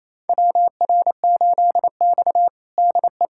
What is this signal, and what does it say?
Morse code, heard as a keyed tone, about 28 wpm, 690 Hz. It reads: WR8X DE
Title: USA Callsign DE Callsign - 4 Characters - 28wpm